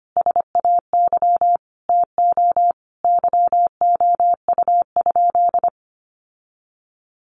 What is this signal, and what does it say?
Morse code, heard as a keyed tone, about 25 wpm, 700 Hz. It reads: SAY TO YOU?
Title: Sentences from Top 800 Words 25wpm